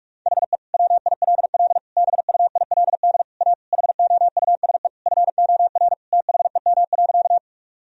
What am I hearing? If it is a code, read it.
Morse code, heard as a keyed tone, about 41 wpm, 700 Hz. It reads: HE WILL BUILD A HOUSE FOR THEM.
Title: Sentences from Top 1000 Words - Review Part 1 - 45wpm